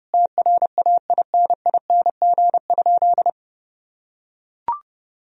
Morse code, heard as a keyed tone, about 32 wpm, 700 Hz. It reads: TRAINING?
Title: Sentences from Top 1000 Words - Review Part 4 - 30wpm